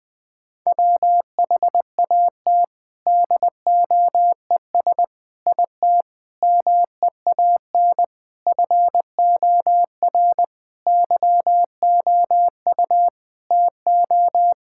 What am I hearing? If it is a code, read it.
Morse code, heard as a keyed tone, about 20 wpm, 700 Hz. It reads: WHAT DOES IT MEAN FOR YOU TO
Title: Sentences from Top 1000 Words - Review Part 6 - 20wpm